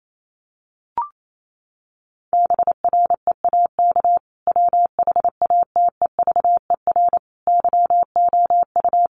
Morse code, heard as a keyed tone, about 28 wpm, 700 Hz. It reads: BREAK WHATEVER YOU
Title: Sentences from Top 1500 Words 28wpm